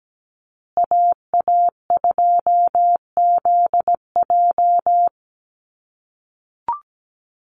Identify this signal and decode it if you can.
Morse code, heard as a keyed tone, about 17 wpm, 700 Hz. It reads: AA2ZJ
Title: USA Call Signs Repeated 2x 17wpm